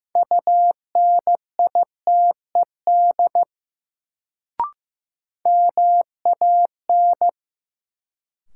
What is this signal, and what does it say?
Morse code, heard as a keyed tone, about 15 wpm, 700 Hz. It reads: UNITED MAN
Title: Sets of 2 Words - New Word Drill - 15wpm